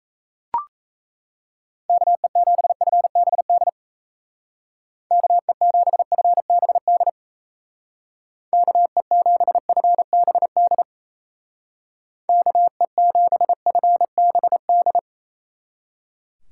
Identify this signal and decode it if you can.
Morse code, heard as a keyed tone, about 33 wpm, 700 Hz. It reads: KE7FBD KE7FBD KE7FBD KE7FBD
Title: Speed Racing - USA 2x3 Call Signs 28wpm